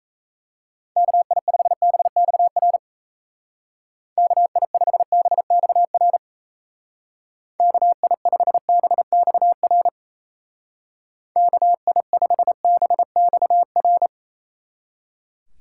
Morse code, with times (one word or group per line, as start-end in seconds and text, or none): 1.0-2.8 KI5BXR
4.2-6.2 KI5BXR
7.6-9.9 KI5BXR
11.4-14.1 KI5BXR